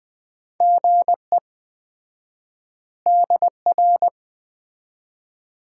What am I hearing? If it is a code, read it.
Morse code, heard as a keyed tone, about 20 wpm, 700 Hz. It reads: GE DR
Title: Common QSO Phrases - 4x Word-Spacing - 20wpm